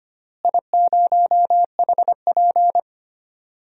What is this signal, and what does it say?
Morse code, heard as a keyed tone, about 25 wpm, 700 Hz. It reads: I0HP
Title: DXCC-100 Call Signs - 4 Characters - Repeated 3x - 25wpm